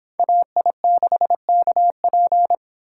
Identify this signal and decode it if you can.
Morse code, heard as a keyed tone, about 26 wpm, 700 Hz. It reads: AI6KP